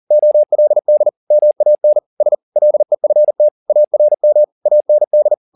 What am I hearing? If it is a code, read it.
Morse code, heard as a keyed tone, about 40 wpm, 600 Hz. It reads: OLD MAN S LEFT ARM AND